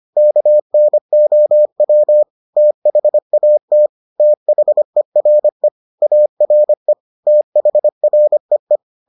Morse code, heard as a keyed tone, about 25 wpm, 600 Hz. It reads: KNOW THAT THERE ARE THREE